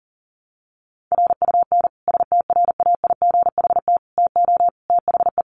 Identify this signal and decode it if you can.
Morse code, heard as a keyed tone, about 40 wpm, 700 Hz. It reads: RUN STRAIGHT TO THE
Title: Sentences from Top 1000 Words 40wpm